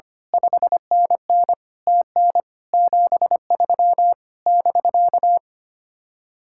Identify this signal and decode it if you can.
Morse code, heard as a keyed tone, about 25 wpm, 700 Hz. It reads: 5NN TN 73 <BK>
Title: Speed-Racing Common QSO Phrases Encore - 25wpm